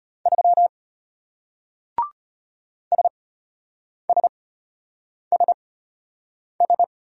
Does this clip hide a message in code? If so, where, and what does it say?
Yes: Morse code from 0.3 to 6.8 s, heard as a keyed tone, about 34 wpm, spaced as for 8 wpm, 690 Hz. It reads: 3 SSSS